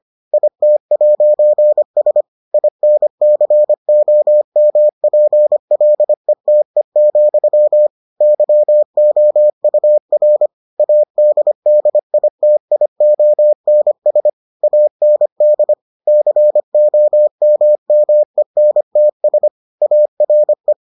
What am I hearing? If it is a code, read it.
Morse code, heard as a keyed tone, about 25 wpm, 600 Hz. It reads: IT'S INCOMPLETE, YOUR ADDITIONS AND COMMENTS ARE